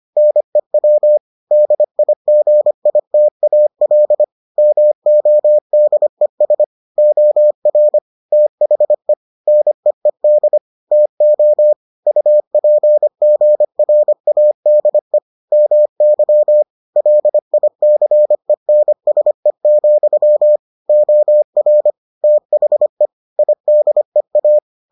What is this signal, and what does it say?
Morse code, heard as a keyed tone, about 25 wpm, 600 Hz. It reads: NEW DIGITAL MODES OR THE NEED TO UPGRADE MY LICENSE, OR THE IDEA